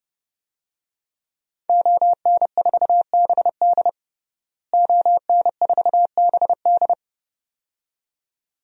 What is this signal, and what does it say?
Morse code, heard as a keyed tone, about 30 wpm, 700 Hz. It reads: ON4BD ON4BD
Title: DXCC-100 Call Signs - 5 Characters - Repeated 2x - 30wpm